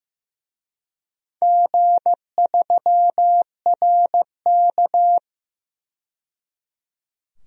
Morse code, heard as a keyed tone, about 15 wpm, 700 Hz. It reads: G3RK
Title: DXCC-100 Call Signs - 4 Characters - 15wpm